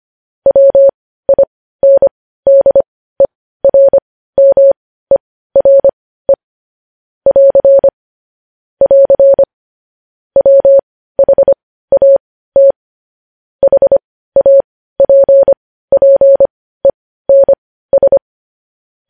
Morse code, heard as a keyed tone, about 25 wpm, spaced as for 15 wpm, 550 Hz. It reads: WINDERMERE <AR> <AR> WHAT HAPPENS
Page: news headlines in morse code at 15 wpm